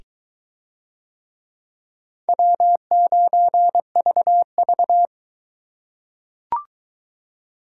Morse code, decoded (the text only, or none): W9VV